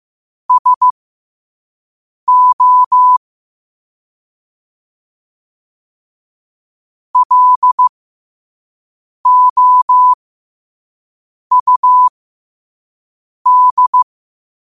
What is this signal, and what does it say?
Morse code, heard as a keyed tone, about 15 wpm, spaced as for 5 wpm, 1000 Hz. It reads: SO LOUD
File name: qotdfiles / 5wpm